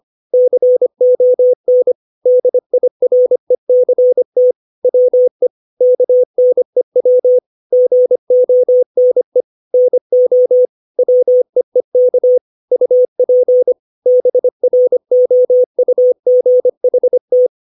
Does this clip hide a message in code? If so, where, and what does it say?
Yes: Morse code from 0.3 to 17.5 s, heard as a keyed tone, about 25 wpm, 500 Hz. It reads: CON DIRECT WE KNEW GONE NO WEEK UP BROUGHT